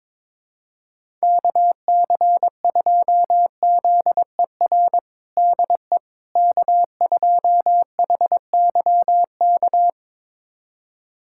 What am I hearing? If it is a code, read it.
Morse code, heard as a keyed tone, about 22 wpm, 700 Hz. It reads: KC2ZER DE K2HYK
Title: USA Callsign DE Callsign 22wpm